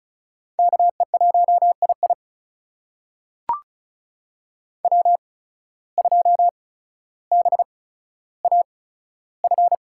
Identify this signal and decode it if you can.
Morse code, heard as a keyed tone, about 36 wpm, spaced as for 15 wpm, 700 Hz. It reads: KE1II W2BAF
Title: USA Call Signs - 5 Characters - 8x Character Spacing - 35wpm